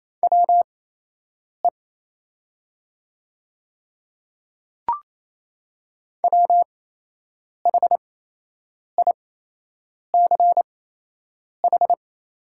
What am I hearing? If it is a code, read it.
Morse code, heard as a keyed tone, about 28 wpm, spaced as for 8 wpm, 690 Hz. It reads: WE WHICH
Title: 8x Character Spacing - Top 100 Words - 28wpm